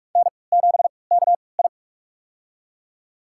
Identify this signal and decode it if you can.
Morse code, heard as a keyed tone, about 42 wpm, spaced as for 25 wpm, 700 Hz. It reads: N7KI
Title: USA Call Signs - 4 Characters - 3x Character Spacing - 45wpm